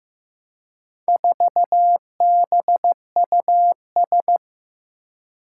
Morse code, heard as a keyed tone, about 15 wpm, 700 Hz. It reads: 4BUS
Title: Four Challenging Letter-Number Combinations 15wpm